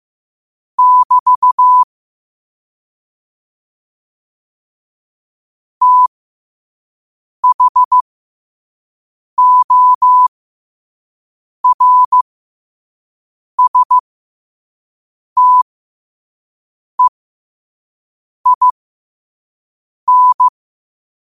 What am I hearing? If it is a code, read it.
Morse code, heard as a keyed tone, about 15 wpm, spaced as for 5 wpm, 1000 Hz. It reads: <BT> THORSTEIN
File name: qotdfiles / 5wpm